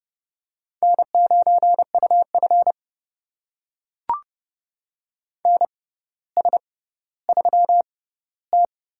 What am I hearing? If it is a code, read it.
Morse code, heard as a keyed tone, about 30 wpm, spaced as for 15 wpm, 700 Hz. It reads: N9UF NS3T
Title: USA Call Signs - 4 Characters - 6x Character Spacing - 30wpm